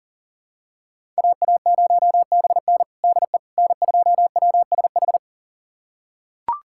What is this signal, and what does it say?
Morse code, heard as a keyed tone, about 40 wpm, 700 Hz. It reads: AA0BN DE N2WSH
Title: USA Callsign DE Callsign - 5 Characters - 40wpm